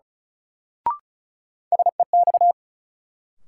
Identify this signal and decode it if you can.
Morse code, heard as a keyed tone, about 34 wpm, 700 Hz. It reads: SEX
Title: Sentences from Top 1600 Words - New Word Drill - 35wpm